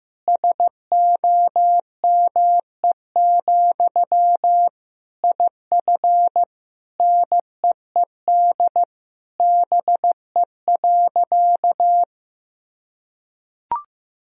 Morse code, heard as a keyed tone, about 14 wpm, 700 Hz. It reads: SOME, IF NEED BE.
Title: Sentences from Top 1000 Words - Review Part 1 - 15wpm